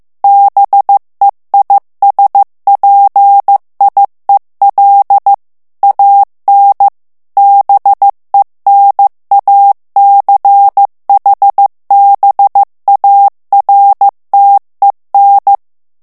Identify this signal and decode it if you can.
Morse code, heard as a keyed tone, about 15 wpm, 800 Hz. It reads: BEISPIEL AN BENACHBARTEN
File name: morse mp3 / ALLEMAND texte clair 1 vitesse 14 mots minute